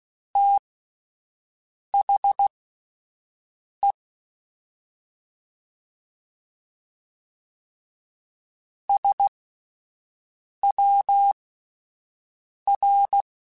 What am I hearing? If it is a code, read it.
Morse code, heard as a keyed tone, about 16 wpm, spaced as for 5 wpm, 800 Hz. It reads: THE SWR
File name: QSOs / QSOs-mp3s / WD0FBT 1@10